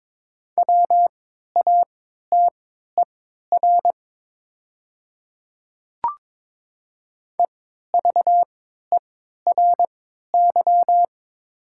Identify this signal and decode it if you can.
Morse code, heard as a keyed tone, about 22 wpm, spaced as for 12 wpm, 700 Hz. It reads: WATER EVERY
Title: 3x Character Spacing - Top 500 Words - 22wpm